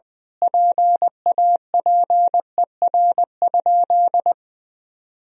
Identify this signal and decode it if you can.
Morse code, heard as a keyed tone, about 21 wpm, 700 Hz. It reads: PAPER?
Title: Sentences from Top 1000 Words - Review Part 2 - 20wpm